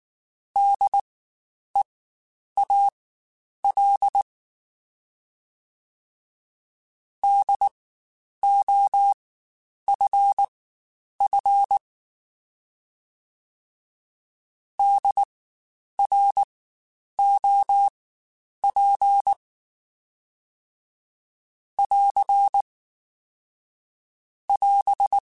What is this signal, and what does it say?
Morse code, heard as a keyed tone, about 19 wpm, spaced as for 9 wpm, 800 Hz. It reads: DEAL DOFF DROP <AR> <AS>
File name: Spaced Alphas / Spaced Alphas mp3s / Ds sp @18